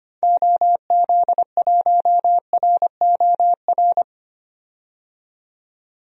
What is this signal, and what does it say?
Morse code, heard as a keyed tone, about 25 wpm, 700 Hz. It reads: OZ1ROR
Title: DXCC-100 Call Signs - 6 Characters - 25wpm